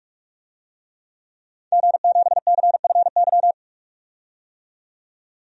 Morse code, heard as a keyed tone, about 44 wpm, 700 Hz. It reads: G7CFY
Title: DXCC-100 Call Signs - 5 Characters - 45wpm